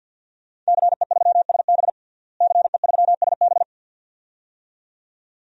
Morse code, heard as a keyed tone, about 48 wpm, 700 Hz. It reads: CE3SB CE3SB
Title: DXCC-100 Call Signs - 5 Characters - Repeated 2x - 50wpm